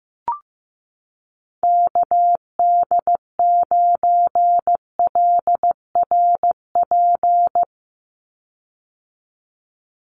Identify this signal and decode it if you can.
Morse code, heard as a keyed tone, about 15 wpm, 700 Hz. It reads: KD9LRP
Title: USA Call Signs 15wpm